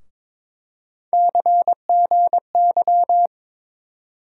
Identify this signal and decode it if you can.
Morse code, heard as a keyed tone, about 22 wpm, 700 Hz. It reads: CGY
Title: USA Call Sign Postfixes - 3 Letters 22wpm